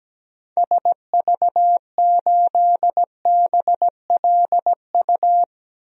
Morse code, heard as a keyed tone, about 17 wpm, 700 Hz. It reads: SV8BLU
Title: DXCC-100 Call Signs - 6 Characters - Repeated 2x - 17wpm